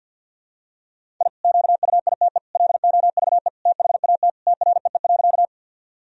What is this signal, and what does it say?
Morse code, heard as a keyed tone, about 51 wpm, 700 Hz. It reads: I QUITE LOVE THAT TREE.
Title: Sentences from Top 1000 Words - Review Part 1 - 50wpm